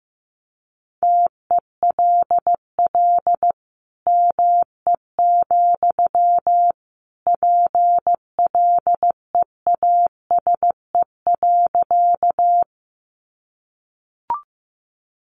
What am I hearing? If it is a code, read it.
Morse code, heard as a keyed tone, about 15 wpm, 700 Hz. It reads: TELL ME, PLEASE.
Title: Sentences from Top 1500 Words 15wpm